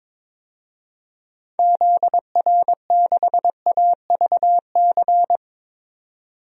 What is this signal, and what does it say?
Morse code, heard as a keyed tone, about 22 wpm, 700 Hz. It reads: ZR6AVC
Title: DXCC-100 Call Signs - 6 Characters - 22wpm